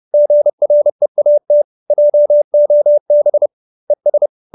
Morse code, heard as a keyed tone, about 30 wpm, 590 Hz. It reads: GREAT JOB ES
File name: QSO321 30